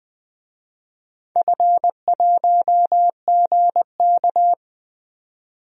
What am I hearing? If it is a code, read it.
Morse code, heard as a keyed tone, about 20 wpm, 700 Hz. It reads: F1GK
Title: DXCC-100 Call Signs - 4 Characters - 20wpm